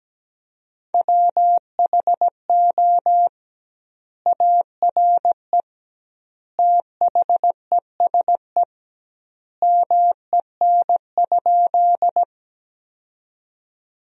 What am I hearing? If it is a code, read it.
Morse code, heard as a keyed tone, about 17 wpm, 700 Hz. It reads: WHO ARE THESE MEN?
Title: 2x Word-Spacing - Sentences from Top 200 Words - 17wpm